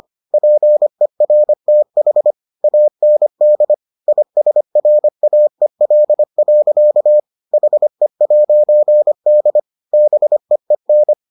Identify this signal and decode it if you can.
Morse code, heard as a keyed tone, about 25 wpm, 600 Hz. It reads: PERTH AND ISRAEL. HE'D BEEN